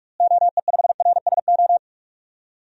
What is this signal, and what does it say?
Morse code, heard as a keyed tone, about 46 wpm, 700 Hz. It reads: OE5RSO
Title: DXCC-100 Call Signs - 6 Characters - Repeated 2x - 45wpm